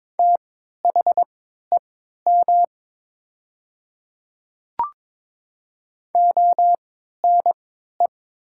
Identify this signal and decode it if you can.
Morse code, heard as a keyed tone, about 22 wpm, spaced as for 12 wpm, 700 Hz. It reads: THEM ONE
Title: 3x Character Spacing - Top 100 Words - 22wpm